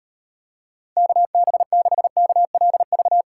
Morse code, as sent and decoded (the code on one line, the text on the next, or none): -.- -... -.... -.- .-.. ...-
KB6KLV